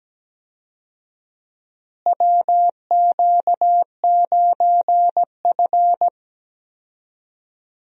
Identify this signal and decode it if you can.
Morse code, heard as a keyed tone, about 17 wpm, 700 Hz. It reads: WQ9F
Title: USA Call Signs - 4 Characters - Repeated 2x - 17wpm